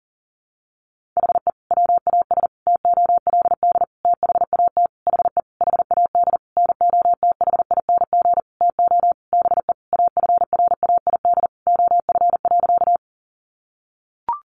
Morse code, heard as a keyed tone, about 40 wpm, 700 Hz. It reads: HE WAS TOLD THAT HE HAD NOTHING TO BE AFRAID OF.
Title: Sentences from Top 1500 Words 40wpm